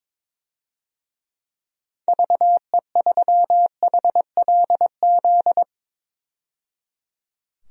Morse code, heard as a keyed tone, about 22 wpm, 700 Hz. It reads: VE3HLZ
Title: DXCC-100 Call Signs - 6 Characters - 22wpm